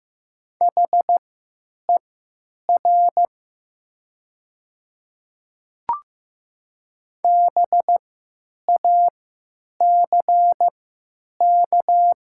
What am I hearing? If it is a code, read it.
Morse code, heard as a keyed tone, about 15 wpm, spaced as for 8 wpm, 700 Hz. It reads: HER BACK
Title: 3x Character Spacing - Top 200 Words - 15wpm